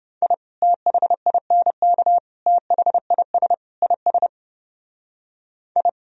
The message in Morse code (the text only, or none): I THINK THIS IS I